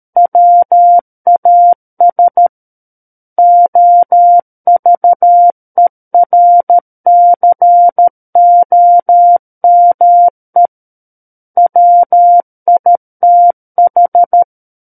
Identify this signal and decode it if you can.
Morse code, heard as a keyed tone, about 13 wpm, 700 Hz. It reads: WAS OVERCOME WITH
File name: Book One / 04-Chapter 4 - 13 WPM